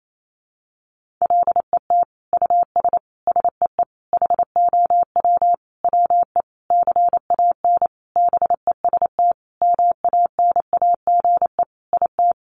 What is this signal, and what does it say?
Morse code, heard as a keyed tone, about 28 wpm, 700 Hz. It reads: LET US SEE HOW WE CAN BEST MANAGE IT
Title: Sentences from Top 1400 Words 28wpm